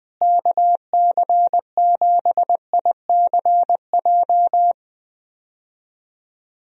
Morse code, heard as a keyed tone, about 20 wpm, 700 Hz. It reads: KC7ICJ
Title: USA Call Signs - 6 Characters - Repeated 3x - 20wpm